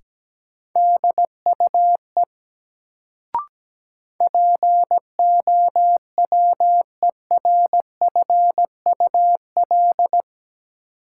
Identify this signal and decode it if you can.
Morse code, heard as a keyed tone, about 17 wpm, 700 Hz. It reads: DUE POWERFUL
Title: Sets of 5 Words Encore - New Word Drill - 17wpm